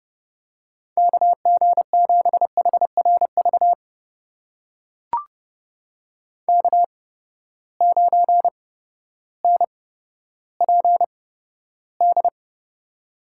Morse code, heard as a keyed tone, about 29 wpm, spaced as for 14 wpm, 700 Hz. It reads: KG7HRV K9NPD
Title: USA Call Signs - 8x Character Spacing - 30wpm